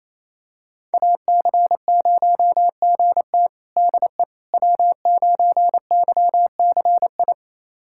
Morse code, heard as a keyed tone, about 27 wpm, 700 Hz. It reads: AC0GT DE W9YCI
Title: USA Callsign DE Callsign - 5 Characters - 28wpm